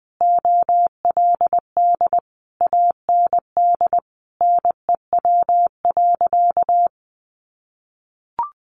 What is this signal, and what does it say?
Morse code, heard as a keyed tone, about 20 wpm, 700 Hz. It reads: OLD AND NEW.
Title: Sentences from Top 1500 Words 20wpm